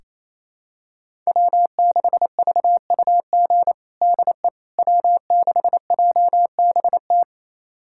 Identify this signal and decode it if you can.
Morse code, heard as a keyed tone, about 28 wpm, 700 Hz. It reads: W6VUG DE W6JBT
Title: USA Callsign DE Callsign - 5 Characters - 28wpm